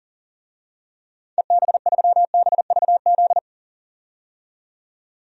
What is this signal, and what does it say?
Morse code, heard as a keyed tone, about 40 wpm, 700 Hz. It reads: EB3BVZ
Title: DXCC-100 Call Signs - 6 Characters - 40wpm